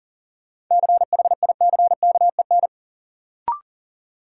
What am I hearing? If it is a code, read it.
Morse code, heard as a keyed tone, about 40 wpm, 700 Hz. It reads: CHICKEN
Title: Sentences from Top 1000 Words - Review Part 5 - New Word Drill - 40wpm